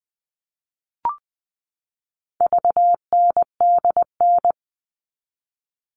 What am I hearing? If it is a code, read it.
Morse code, heard as a keyed tone, about 20 wpm, 700 Hz. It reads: VNDN
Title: All Cut Numbers - 4-Digit 20wpm